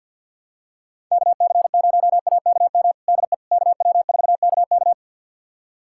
Morse code, heard as a keyed tone, about 46 wpm, 700 Hz. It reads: KC0AKM DE KW4KK
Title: USA Callsign DE Callsign 50wpm